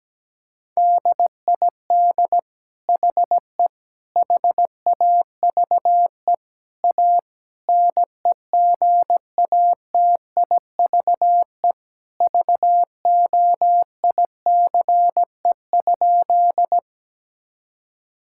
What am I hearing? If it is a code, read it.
Morse code, heard as a keyed tone, about 17 wpm, 700 Hz. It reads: DID HE HAVE A NEGATIVE VOICE?
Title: Sentences from Top 1000 Words - Review Part 6 - 17wpm